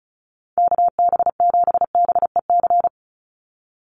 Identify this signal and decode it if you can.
Morse code, heard as a keyed tone, about 34 wpm, 700 Hz. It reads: KB7BEC
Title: USA Call Signs 35wpm